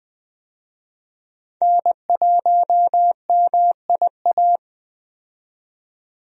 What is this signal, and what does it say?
Morse code, heard as a keyed tone, about 21 wpm, 700 Hz. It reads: N1MIA
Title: USA Call Signs - 5 Characters - 20wpm